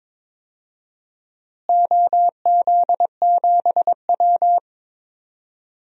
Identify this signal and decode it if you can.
Morse code, heard as a keyed tone, about 23 wpm, 700 Hz. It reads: OZ7W
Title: DXCC-100 Call Signs - 4 Characters - Repeated 3x - 22wpm